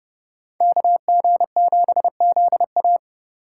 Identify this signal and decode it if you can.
Morse code, heard as a keyed tone, about 30 wpm, 700 Hz. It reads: KG7ZA